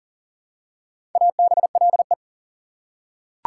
Morse code, heard as a keyed tone, about 40 wpm, 700 Hz. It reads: ABLE
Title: Prefixes and Suffixes 40wpm